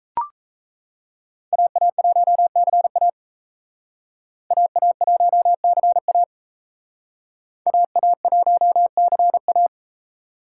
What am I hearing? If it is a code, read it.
Morse code, heard as a keyed tone, about 36 wpm, 700 Hz. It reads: AA1CA AA1CA AA1CA